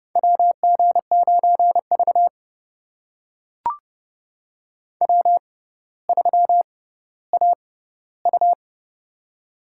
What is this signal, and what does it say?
Morse code, heard as a keyed tone, about 30 wpm, spaced as for 15 wpm, 690 Hz. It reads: WG9V W3AU
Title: USA Call Signs - 4 Characters - 6x Character Spacing - 30wpm